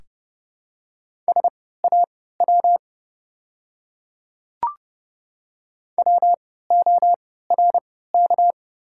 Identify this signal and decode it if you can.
Morse code, heard as a keyed tone, about 30 wpm, spaced as for 17 wpm, 700 Hz. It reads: SAW WORK
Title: 3x Character Spacing - Top 300 Words - 30wpm